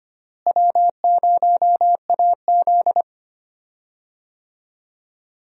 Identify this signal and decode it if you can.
Morse code, heard as a keyed tone, about 25 wpm, 700 Hz. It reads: W0AZ
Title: USA Call Signs - 4 Characters - Repeated 3x - 25wpm